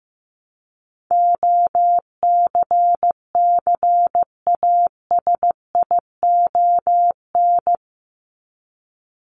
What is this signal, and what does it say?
Morse code, heard as a keyed tone, about 15 wpm, 700 Hz. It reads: OCCASION
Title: Top 1501-1750 Words 15wpm